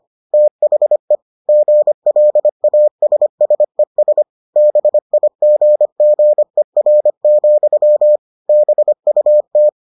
Morse code, heard as a keyed tone, about 25 wpm, 600 Hz. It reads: THE GLASSES BIGGER, BUT